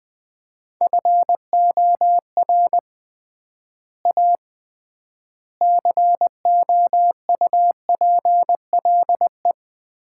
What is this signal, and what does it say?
Morse code, heard as a keyed tone, about 20 wpm, 700 Hz. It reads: FOR A COUPLE
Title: 3x Word-Spacing - Sets of 4 Words - 20wpm